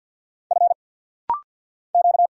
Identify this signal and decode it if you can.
Morse code, heard as a keyed tone, about 48 wpm, 690 Hz. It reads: F Q
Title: Single Letter-Number - Rapid-Fire 50wpm